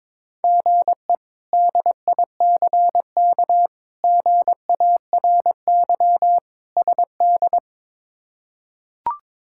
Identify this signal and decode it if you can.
Morse code, heard as a keyed tone, about 22 wpm, 700 Hz. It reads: GE DICK GARY SD
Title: K1USN SST QSO Practice - 22wpm